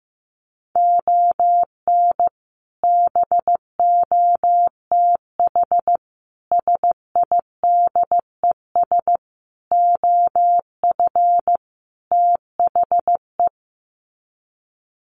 Morse code, heard as a keyed tone, about 15 wpm, 700 Hz. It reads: ON BOTH SIDES OF THE
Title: Sets of 5 Words 15wpm